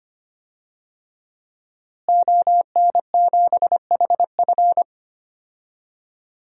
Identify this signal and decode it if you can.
Morse code, heard as a keyed tone, about 25 wpm, 700 Hz. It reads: ON7HF
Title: DXCC-100 Call Signs - 5 Characters - Repeated 3x - 25wpm